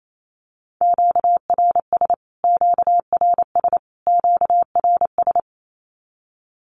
Morse code, heard as a keyed tone, about 28 wpm, 700 Hz. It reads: QRS QRS QRS
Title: Q-Codes Repeated 3x 28wpm